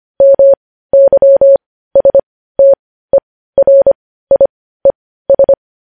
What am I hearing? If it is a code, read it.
Morse code, heard as a keyed tone, about 25 wpm, spaced as for 15 wpm, 550 Hz. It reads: MYSTERIES
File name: podcast / News headlines-1x-25x15